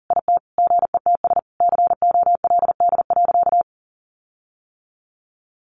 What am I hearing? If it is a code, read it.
Morse code, heard as a keyed tone, about 40 wpm, 700 Hz. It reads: IT GETS COLD.
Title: Sentences from Top 1300 Words 40wpm